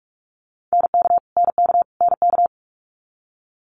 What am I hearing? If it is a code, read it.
Morse code, heard as a keyed tone, about 46 wpm, 700 Hz. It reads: NK NK NK
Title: Multi-letter Phonemes Repeated 3x 45wpm